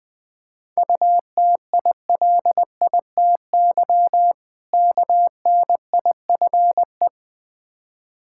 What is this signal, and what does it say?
Morse code, heard as a keyed tone, about 20 wpm, 700 Hz. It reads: UTILITY KNIFE
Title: Difficult Words with Hints - 20wpm